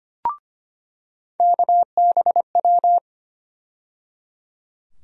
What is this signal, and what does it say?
Morse code, heard as a keyed tone, about 25 wpm, 700 Hz. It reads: KBW
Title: USA Call Sign Postfixes - 3 Letters 25wpm